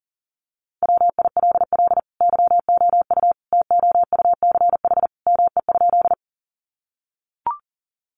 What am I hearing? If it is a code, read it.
Morse code, heard as a keyed tone, about 40 wpm, 700 Hz. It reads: WILL YOU TOUCH ME?
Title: Sentences from Top 1200 Words 40wpm